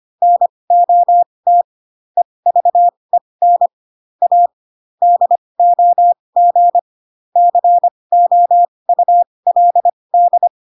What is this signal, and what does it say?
Morse code, heard as a keyed tone, about 25 wpm, spaced as for 20 wpm, 700 Hz. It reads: NOT EVEN A DOG COULD
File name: Pinocchio / Pinocchio ch6 15 25 ik1hge